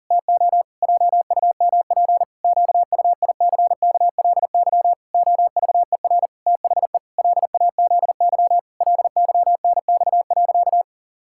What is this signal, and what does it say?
Morse code, heard as a keyed tone, about 37 wpm, 700 Hz. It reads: TO JUMP QUICKLY OVER THE LAZY LYNX.
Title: Pangram Challenge v2 - 40wpm